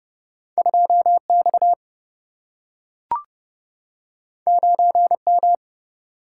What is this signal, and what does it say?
Morse code, heard as a keyed tone, about 30 wpm, 700 Hz. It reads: 2 X 9 M
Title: Single-Letter Single-Number 30wpm